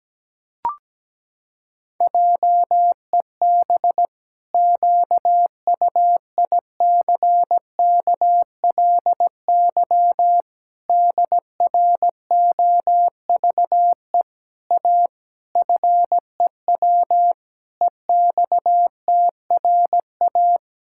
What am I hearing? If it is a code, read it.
Morse code, heard as a keyed tone, about 17 wpm, 700 Hz. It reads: JEB QUICKLY DROVE A FEW EXTRA
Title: Pangram Challenge v2 - 17wpm